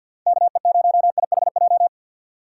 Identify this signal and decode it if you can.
Morse code, heard as a keyed tone, about 49 wpm, 700 Hz. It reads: KE0IHJ
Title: USA Call Signs - 6 Characters - 3x Character Spacing - 50wpm